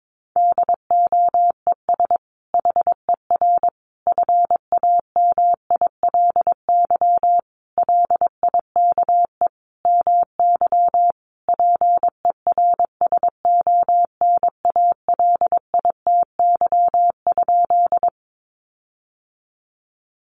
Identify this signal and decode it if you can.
Morse code, heard as a keyed tone, about 22 wpm, 700 Hz. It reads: DOES HER FAMILY LIKE MY PERSONALITY?
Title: Sentences from Top 1400 Words 22wpm